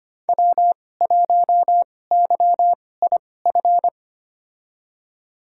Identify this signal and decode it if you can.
Morse code, heard as a keyed tone, about 25 wpm, spaced as for 18 wpm, 700 Hz. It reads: W1YIF
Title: USA Call Signs - 5 Characters - 2x Character Spacing - 25wpm